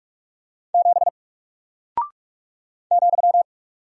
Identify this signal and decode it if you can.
Morse code, heard as a keyed tone, about 48 wpm, 700 Hz. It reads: 7 ,
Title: Single Character (Letter:Number:Punctuation:Prosign) Alt 45wpm